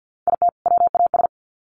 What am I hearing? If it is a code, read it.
Morse code, heard as a keyed tone, about 50 wpm, 700 Hz. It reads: IT WAS
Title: Sets of 2 Words 50wpm